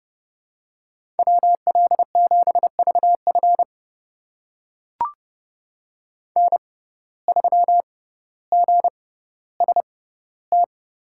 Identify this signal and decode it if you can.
Morse code, heard as a keyed tone, about 32 wpm, spaced as for 16 wpm, 700 Hz. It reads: WL7VF N3GST
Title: USA Call Signs - 5 Characters - 6x Character Spacing - 30wpm